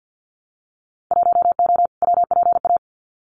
Morse code, heard as a keyed tone, about 50 wpm, 700 Hz. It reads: 1O WPA
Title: Winter Field Day Exchanges 50wpm